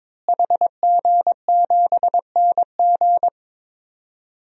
Morse code, heard as a keyed tone, about 21 wpm, 700 Hz. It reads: HG7NG
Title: DXCC-100 Call Signs - 5 Characters - Repeated 2x - 22wpm